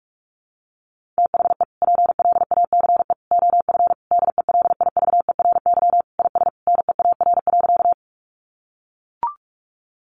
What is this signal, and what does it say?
Morse code, heard as a keyed tone, about 44 wpm, 700 Hz. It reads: THE PLACE OF DELIVERY IS NEAR.
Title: Sentences from Top 1500 Words 45wpm